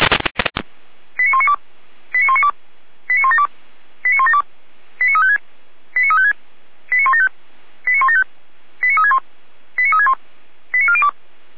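Five-tone selective call (ZVEI), heard as quick runs of five tones, tones of about 70 ms. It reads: 89192 89192 89172 89172 89356 89356 89175 89175 89361 89361 89402
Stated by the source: ZVEI five-tone selective call received by SDR, address 89018.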